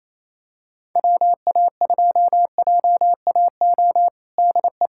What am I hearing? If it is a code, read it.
Morse code, heard as a keyed tone, about 28 wpm, 700 Hz. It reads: WA2JAO DE
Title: USA Callsign DE Callsign 28wpm